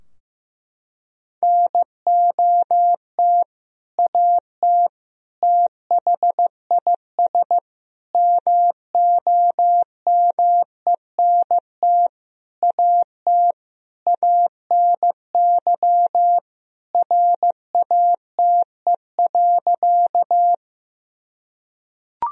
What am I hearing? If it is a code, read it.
Morse code, heard as a keyed tone, about 15 wpm, 700 Hz. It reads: NOT AT THIS MOMENT AT ANY RATE.
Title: Sentences from Top 1000 Words - Review Part 4 - 15wpm